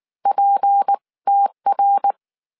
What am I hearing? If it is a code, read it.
Morse code, heard as a keyed tone, about 19 wpm, spaced as for 16 wpm, 800 Hz. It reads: PTR
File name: CW / DET / 1 6 1500C S C 18wpm 2356